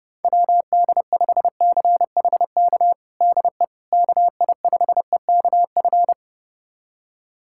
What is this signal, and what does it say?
Morse code, heard as a keyed tone, about 30 wpm, 700 Hz. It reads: WD5CHK DE KI5EKF
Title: USA Callsign DE Callsign 30wpm